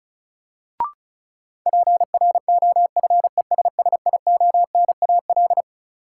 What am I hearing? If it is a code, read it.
Morse code, heard as a keyed tone, about 36 wpm, 700 Hz. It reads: PROFESSIONAL
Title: Sentences from Top 1000 Words - Review Part 4 - New Word Drill - 35wpm